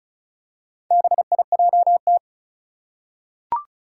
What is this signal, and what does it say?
Morse code, heard as a keyed tone, about 36 wpm, 700 Hz. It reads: BIJT
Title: Four Challenging Letter-Number Combinations 35wpm